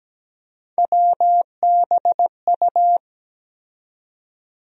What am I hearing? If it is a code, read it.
Morse code, heard as a keyed tone, about 17 wpm, 700 Hz. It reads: WBU
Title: USA Call Sign Postfixes - 3 Letters 17wpm